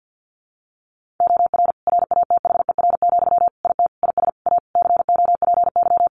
Morse code, heard as a keyed tone, about 50 wpm, 700 Hz. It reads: OR RATHER, IT IS A COPY
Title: Sentences from Top 1400 Words 50wpm